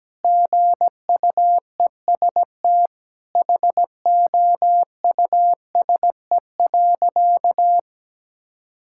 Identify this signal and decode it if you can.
Morse code, heard as a keyed tone, about 17 wpm, 700 Hz. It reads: GUEST HOUSE.
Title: Sentences from Top 1600 Words 17wpm